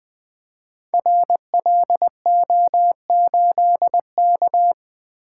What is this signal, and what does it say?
Morse code, heard as a keyed tone, about 20 wpm, 700 Hz. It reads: RLO8K
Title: Code Groups - 5 Letter:Numbers - 20wpm